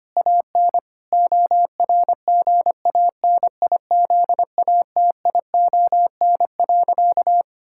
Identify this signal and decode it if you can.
Morse code, heard as a keyed tone, about 25 wpm, 700 Hz. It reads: AN ORGANIZATION.
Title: Sentences from Top 1000 Words - Review Part 5 - 25wpm